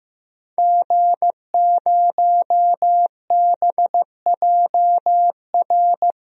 Morse code, heard as a keyed tone, about 15 wpm, 700 Hz. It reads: G0BJR
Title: DXCC-100 Call Signs - 5 Characters - Repeated 2x - 15wpm